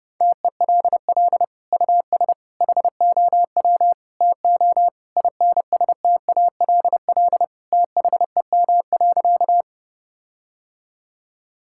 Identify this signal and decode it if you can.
Morse code, heard as a keyed tone, about 32 wpm, 700 Hz. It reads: TELL US HOW TO INSTALL THEM.
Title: Sentences from Top 1600 Words 30wpm